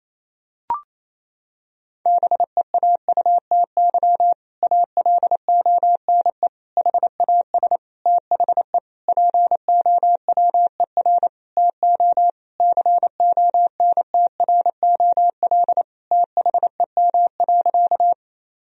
Morse code, heard as a keyed tone, about 27 wpm, 700 Hz. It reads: BEAUTY ALONE HAS THE POWER TO CONTROL THEM.
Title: Sentences from Top 1000 Words - Review Part 2 - 28wpm